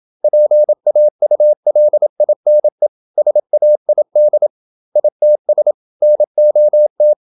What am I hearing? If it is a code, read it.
Morse code, heard as a keyed tone, about 27 wpm, 600 Hz. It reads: PAULINE SAID ITS NOT
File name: Never 0027wpm